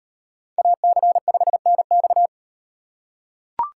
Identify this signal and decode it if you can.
Morse code, heard as a keyed tone, about 37 wpm, 700 Hz. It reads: AC5NX